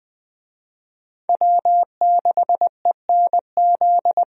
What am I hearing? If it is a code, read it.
Morse code, heard as a keyed tone, about 20 wpm, 700 Hz. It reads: W6ENZ
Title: USA Call Signs - 5 Characters - Repeated 3x - 20wpm